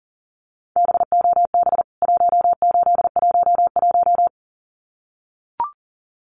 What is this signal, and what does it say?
Morse code, heard as a keyed tone, about 40 wpm, 700 Hz. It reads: BOB 1811
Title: CWT Exchange 40wpm